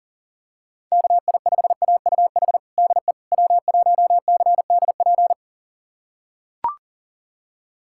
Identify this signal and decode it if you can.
Morse code, heard as a keyed tone, about 40 wpm, 700 Hz. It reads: KI5AUH DE W1CDP
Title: USA Callsign DE Callsign 40wpm